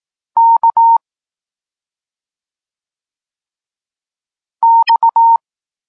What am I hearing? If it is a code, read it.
Morse code, heard as a keyed tone, about 18 wpm, 920 Hz. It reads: K X
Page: Tekst 32 – Doskonalenie odbioru poznanych liter